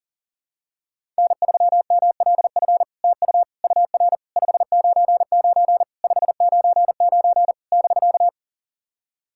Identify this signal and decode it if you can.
Morse code, heard as a keyed tone, about 40 wpm, 700 Hz. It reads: N3MLF TU UR 599 599 <BK>
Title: SOTA QSO Practice 40wpm